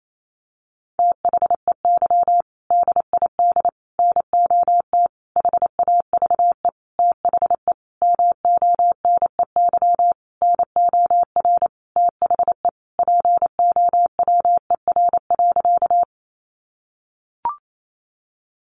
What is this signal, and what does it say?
Morse code, heard as a keyed tone, about 28 wpm, 700 Hz. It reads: THEY DID NOT HAVE THE MONEY NOR THE POWER.
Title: Sentences from Top 1500 Words 28wpm